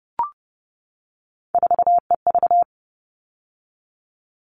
Morse code, heard as a keyed tone, about 30 wpm, 700 Hz. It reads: 4EV
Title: All Cut Numbers - 3-Digit 30wpm